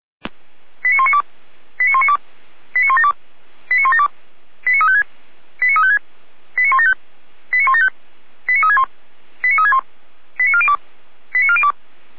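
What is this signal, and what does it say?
Five-tone selective call (ZVEI), heard as quick runs of five tones, tones of about 70 ms. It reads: 89192 89192 89172 89172 89356 89356 89175 89175 89361 89361 89402 89402
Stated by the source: ZVEI five-tone selective call received on an SDR.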